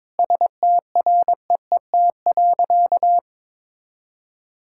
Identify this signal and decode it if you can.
Morse code, heard as a keyed tone, about 22 wpm, 700 Hz. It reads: STREET.
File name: practice / 22wpm / Sentences with 3-Word Phrases - 22wpm